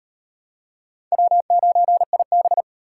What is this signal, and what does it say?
Morse code, heard as a keyed tone, about 34 wpm, 700 Hz. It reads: W9IB